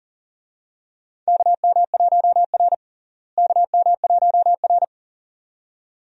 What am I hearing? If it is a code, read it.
Morse code, heard as a keyed tone, about 37 wpm, 700 Hz. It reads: KM1R KM1R
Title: USA Call Signs - 4 Characters - Repeated 2x - 40wpm